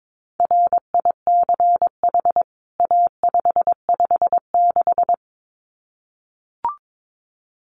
Morse code, heard as a keyed tone, about 22 wpm, 700 Hz. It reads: RICH A556
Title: CWT Exchange 22wpm